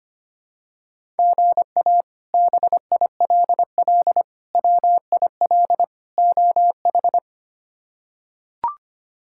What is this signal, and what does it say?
Morse code, heard as a keyed tone, about 25 wpm, 700 Hz. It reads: GA BILL WIL OH
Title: K1USN SST QSO Practice - 25wpm